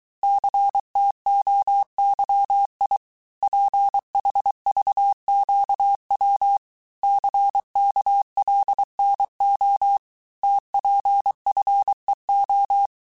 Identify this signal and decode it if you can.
Morse code, heard as a keyed tone, about 23 wpm, 800 Hz. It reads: CTOYI PHVQW CKLNO TPFEO
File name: cours-du-mercredi / audios / cours-855-texte-chiffre-vitesse-22-mpm